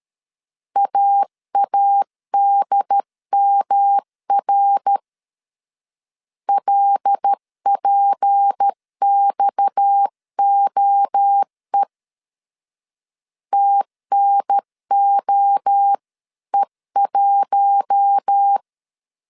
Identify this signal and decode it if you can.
Morse code, heard as a keyed tone, about 13 wpm, 800 Hz. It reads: AADMR LPXOE TNOE1
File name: CW / DET / 1 6 1500C S C 12wpm 3533